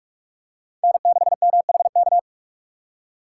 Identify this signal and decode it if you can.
Morse code, heard as a keyed tone, about 44 wpm, 700 Hz. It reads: N6MHK